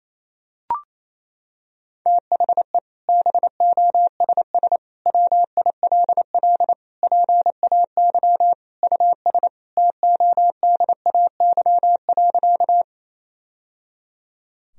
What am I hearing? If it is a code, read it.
Morse code, heard as a keyed tone, about 28 wpm, 700 Hz. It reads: THE BOSS WILL PAY US TODAY.
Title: Sentences from Top 1000 Words - Review Part 4 - 28wpm